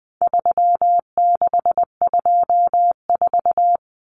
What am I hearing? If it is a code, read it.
Morse code, heard as a keyed tone, about 19 wpm, 700 Hz. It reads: 3624
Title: Common Cut Numbers - 4-Digit 20wpm